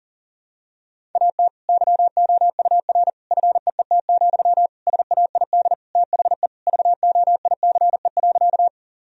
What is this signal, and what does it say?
Morse code, heard as a keyed tone, about 40 wpm, 700 Hz. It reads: AT YOUR FEET, SAID THE VOICE.
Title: Sentences from Top 1000 Words - Review Part 1 - 40wpm